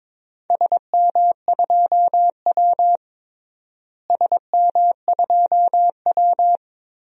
Morse code, heard as a keyed tone, about 22 wpm, 700 Hz. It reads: SM2W SM2W
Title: DXCC-100 Call Signs - 4 Characters - Repeated 3x - 22wpm